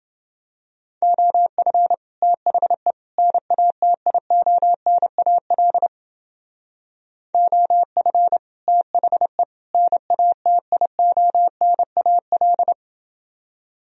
Morse code, heard as a keyed tone, about 30 wpm, 700 Hz. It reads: OF THE NATIONAL OF THE NATIONAL